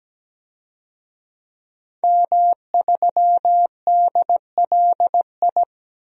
Morse code, heard as a keyed tone, about 17 wpm, 700 Hz. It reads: M3DLI
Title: DXCC-100 Call Signs - 5 Characters - 17wpm